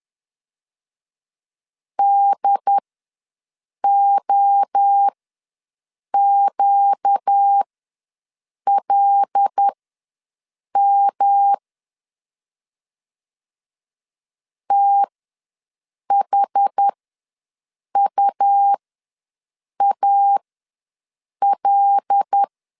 Morse code, heard as a keyed tone, about 10 wpm, spaced as for 6 wpm, 800 Hz. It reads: DOQLM THUAL
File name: LEZIONI / 5 LBVC / 5 L ws 10wpm 3636